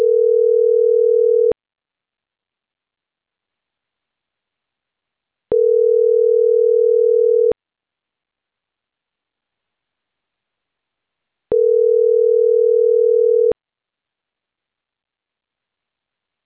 Ringback tone (North America), 440 + 480 Hz.
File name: CENTRAL CAEA / out agcaea10 - .85866